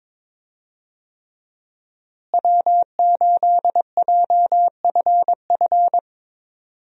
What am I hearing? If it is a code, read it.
Morse code, heard as a keyed tone, about 22 wpm, 700 Hz. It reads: W8JFF
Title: USA Call Signs - 5 Characters - Repeated 3x - 22wpm